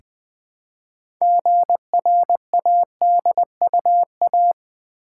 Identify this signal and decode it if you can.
Morse code, heard as a keyed tone, about 20 wpm, 700 Hz. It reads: GRADUA
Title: Build A Word - 3 Letters At A Time - 9-Letter Words - 20wpm